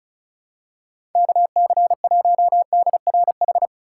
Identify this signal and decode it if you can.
Morse code, heard as a keyed tone, about 36 wpm, 700 Hz. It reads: KC1DRH
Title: USA Call Signs - 6 Characters - 2x Character Spacing - 35wpm